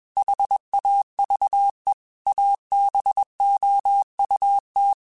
Morse code, heard as a keyed tone, about 21 wpm, 800 Hz. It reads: HAVE ABOUT
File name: QSOs / QSOs-mp3s / WA7QOX 1@20